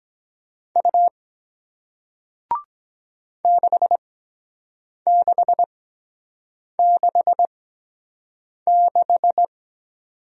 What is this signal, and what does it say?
Morse code, heard as a keyed tone, about 23 wpm, spaced as for 7 wpm, 700 Hz. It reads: U 6666